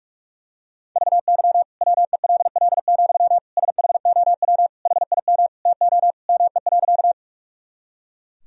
Morse code, heard as a keyed tone, about 43 wpm, 700 Hz. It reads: VY WELL, SHOW HIM TO ME.
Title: Sentences from Top 200 Words and Abbreviations 45wpm